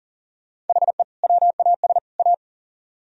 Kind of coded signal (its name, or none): Morse code